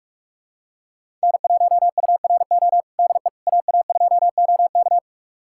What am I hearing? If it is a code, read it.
Morse code, heard as a keyed tone, about 46 wpm, 700 Hz. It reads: N1URO DE AA2OK
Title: USA Callsign DE Callsign - 5 Characters - 45wpm